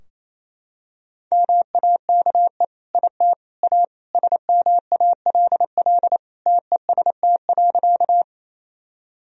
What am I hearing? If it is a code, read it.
Morse code, heard as a keyed tone, about 28 wpm, 700 Hz. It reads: MAKE IT A SMALL TEST.
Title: Sentences from Top 1000 Words - Review Part 3 - 28wpm